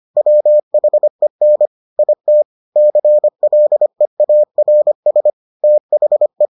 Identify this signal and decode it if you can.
Morse code, heard as a keyed tone, about 25 wpm, 600 Hz. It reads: WHEN IT CLEARS THE